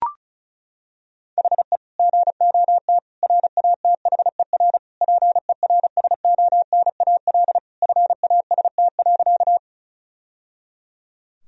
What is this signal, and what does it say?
Morse code, heard as a keyed tone, about 34 wpm, 700 Hz. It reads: HE GOT RATHER PERSONAL FAST.
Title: Sentences from Top 1000 Words - Review Part 4 - 35wpm